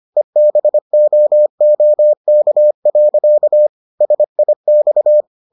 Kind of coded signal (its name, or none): Morse code